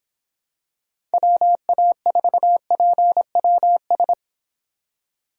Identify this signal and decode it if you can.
Morse code, heard as a keyed tone, about 26 wpm, 700 Hz. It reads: WA4PWS